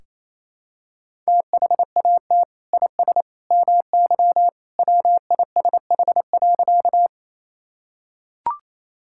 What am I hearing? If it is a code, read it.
Morse code, heard as a keyed tone, about 28 wpm, 700 Hz. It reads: THAT IS MY WISH.
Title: Sentences from Top 1600 Words 28wpm